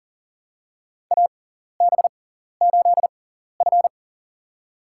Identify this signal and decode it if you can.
Morse code, heard as a keyed tone, about 44 wpm, spaced as for 14 wpm, 700 Hz. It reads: AB8F
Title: USA Call Signs - 4 Characters - 6x Character Spacing - 40wpm